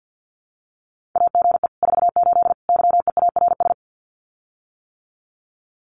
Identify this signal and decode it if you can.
Morse code, heard as a keyed tone, about 48 wpm, 700 Hz. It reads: AGE 48 YEARS
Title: Common QSO Phrases Encore 50wpm